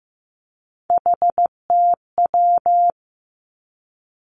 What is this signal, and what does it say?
Morse code, heard as a keyed tone, about 15 wpm, 700 Hz. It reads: HTW
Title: Three Challenging Letter-Number Combinations 15wpm